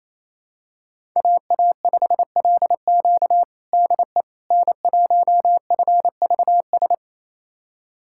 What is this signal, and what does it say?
Morse code, heard as a keyed tone, about 28 wpm, 700 Hz. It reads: AA5LQ DE N1FVS
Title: USA Callsign DE Callsign - 5 Characters - 28wpm